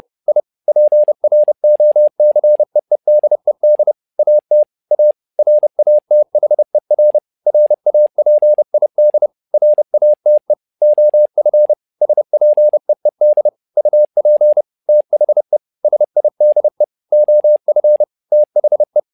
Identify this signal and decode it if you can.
Morse code, heard as a keyed tone, about 30 wpm, 600 Hz. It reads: I PROCEEDED AT A RATHER RAPID RATE OF SPEED UP THE SIDE OF THE
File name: learning center / tower trouble / tower trouble 30wpm